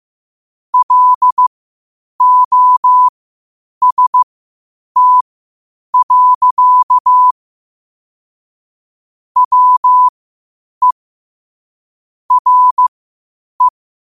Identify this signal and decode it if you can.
Morse code, heard as a keyed tone, about 15 wpm, spaced as for 8 wpm, 1000 Hz. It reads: LOST. WE RE
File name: qotdfiles / 8wpm